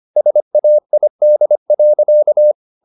Morse code, heard as a keyed tone, about 25 wpm, 600 Hz. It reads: SAID.
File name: the old man and the sea part 1 25wpm